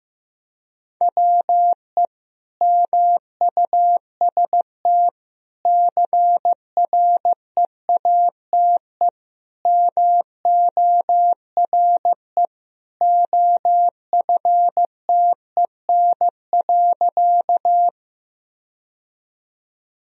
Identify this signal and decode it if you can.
Morse code, heard as a keyed tone, about 15 wpm, 700 Hz. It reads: WE MUST CREATE MORE OFTEN.
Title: Sentences from Top 1000 Words - Review Part 3 - 15wpm